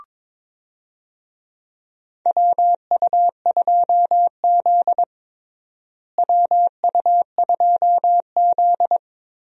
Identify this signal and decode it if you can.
Morse code, heard as a keyed tone, about 21 wpm, 700 Hz. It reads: WU2Z WU2Z
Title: USA Call Signs - 4 Characters - Repeated 3x - 22wpm